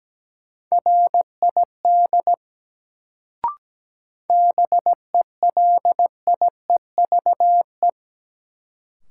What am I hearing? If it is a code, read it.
Morse code, heard as a keyed tone, about 17 wpm, 700 Hz. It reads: RID BELIEVE
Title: Sets of 4 Words - New Word Drill - 17wpm